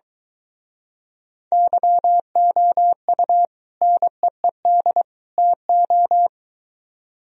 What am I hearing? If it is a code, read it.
Morse code, heard as a keyed tone, about 23 wpm, 700 Hz. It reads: YOU NEED TO